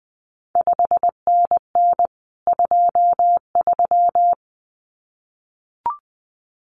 Morse code, heard as a keyed tone, about 20 wpm, 700 Hz. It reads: 5NN 23
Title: CQ WW DX Contest Exchange Encore - 20wpm